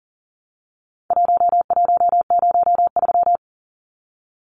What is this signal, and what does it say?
Morse code, heard as a keyed tone, about 40 wpm, 700 Hz. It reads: 1103
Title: Numbers - Contest Style - 4-Digit 40wpm